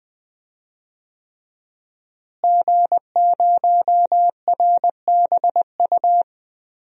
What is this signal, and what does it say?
Morse code, heard as a keyed tone, about 21 wpm, 700 Hz. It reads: G0RBU
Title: DXCC-100 Call Signs - 5 Characters - Repeated 3x - 20wpm